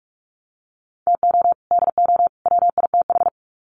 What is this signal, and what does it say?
Morse code, heard as a keyed tone, about 44 wpm, 700 Hz. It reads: TO DO WITH
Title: Sets of 3 Words 45wpm